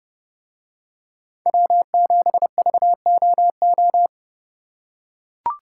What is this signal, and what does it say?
Morse code, heard as a keyed tone, about 29 wpm, 700 Hz. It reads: W7VOO